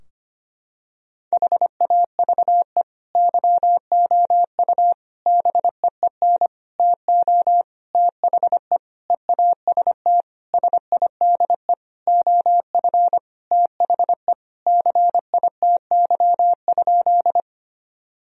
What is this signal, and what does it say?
Morse code, heard as a keyed tone, about 25 wpm, 700 Hz. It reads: HAVE YOU BEEN TO THE EAST SIDE OF THE CITY?
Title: Sentences from Top 1000 Words - Review Part 3 - 25wpm